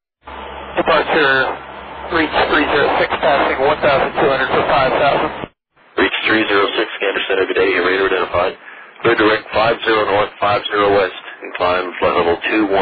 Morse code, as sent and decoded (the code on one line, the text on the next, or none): none